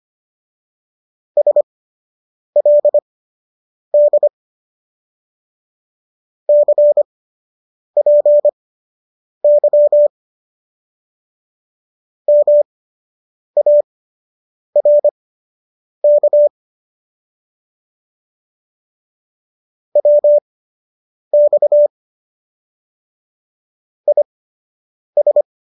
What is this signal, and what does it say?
Morse code, heard as a keyed tone, about 25 wpm, spaced as for 8 wpm, 590 Hz. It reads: SLD CPY MARK WX IS